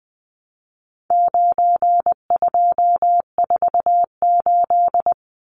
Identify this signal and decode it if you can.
Morse code, heard as a keyed tone, about 20 wpm, 700 Hz. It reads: 9248
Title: Numbers - 4-Digit 20wpm